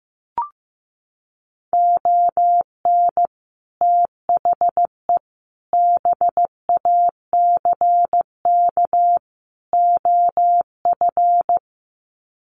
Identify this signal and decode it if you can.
Morse code, heard as a keyed tone, about 15 wpm, 700 Hz. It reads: ON THE BACK OF
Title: Sets of 4 Words 15wpm